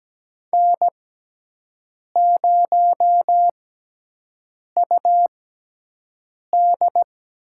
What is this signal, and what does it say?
Morse code, heard as a keyed tone, about 17 wpm, spaced as for 6 wpm, 700 Hz. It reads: N0UD
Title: USA Call Signs - 5 Characters - 6x Character Spacing - 17wpm